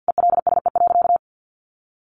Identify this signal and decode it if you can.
Morse code, heard as a keyed tone, about 50 wpm, 700 Hz. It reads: ELSE.
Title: Sentences from Top 800 Words 50wpm